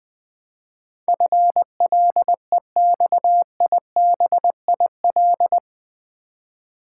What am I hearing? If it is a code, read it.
Morse code, heard as a keyed tone, about 20 wpm, 700 Hz. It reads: FLEXIBIL
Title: Build A Word - 4 Letters At A Time - 11-Letter Words - 20wpm